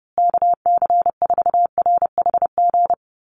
Morse code, heard as a keyed tone, about 30 wpm, 700 Hz. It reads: KC4RHG
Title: USA Call Signs Repeated 2x 30wpm